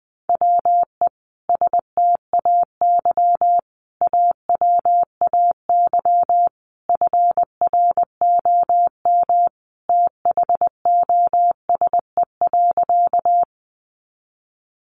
Morse code, heard as a keyed tone, about 20 wpm, 700 Hz. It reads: WE STAY AWAY FROM THOSE.
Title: A Sound Of Thunder 20wpm